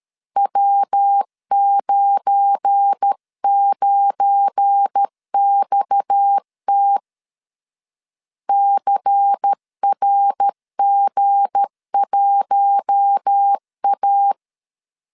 Morse code, heard as a keyed tone, about 13 wpm, 800 Hz. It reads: W99XT CRG1A
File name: LEZIONI / 7 GF9Z / 7 GF9 12wpm 2319